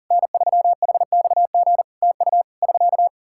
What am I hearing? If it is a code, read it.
Morse code, heard as a keyed tone, about 40 wpm, 700 Hz. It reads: N3HXG TU <SK>
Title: Practice QSO - Exchange 4 - Ending - 40wpm